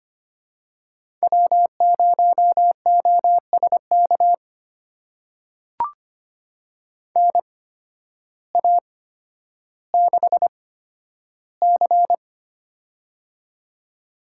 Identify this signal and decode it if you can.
Morse code, heard as a keyed tone, about 25 wpm, spaced as for 12 wpm, 700 Hz. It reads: W0OSK NA6C
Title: USA Call Signs - 8x Character Spacing - 25wpm